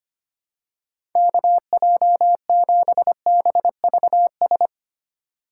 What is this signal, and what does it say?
Morse code, heard as a keyed tone, about 25 wpm, 700 Hz. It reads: KJ7BVS
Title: USA Call Signs - 6 Characters - 3x Character Spacing - 25wpm